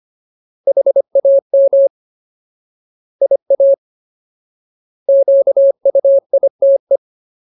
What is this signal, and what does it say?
Morse code, heard as a keyed tone, about 25 wpm, 550 Hz. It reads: HAM IA QUITE
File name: Ultimate Word List / 25WPM Ultimate